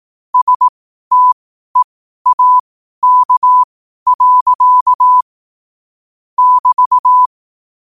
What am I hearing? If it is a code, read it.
Morse code, heard as a keyed tone, about 18 wpm, spaced as for 12 wpm, 1000 Hz. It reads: STEAK. <BT>
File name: qotdfiles / 12wpm